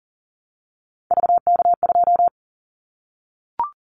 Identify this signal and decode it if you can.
Morse code, heard as a keyed tone, about 40 wpm, 700 Hz. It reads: VK2
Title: Callsign Prefixes for Top 50 DXCC - 40wpm